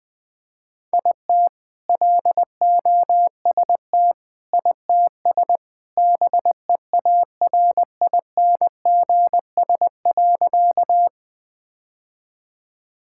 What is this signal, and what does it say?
Morse code, heard as a keyed tone, about 20 wpm, 700 Hz. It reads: IT LOST ITS BEARINGS.
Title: All-Age Appropriate Jokes - 20wpm